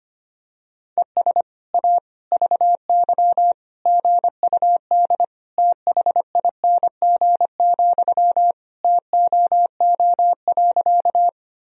Morse code, heard as a keyed tone, about 25 wpm, 700 Hz. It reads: ES A VY GUD THING, TOO.
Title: Sentences from Top 200 Words and Abbreviations 25wpm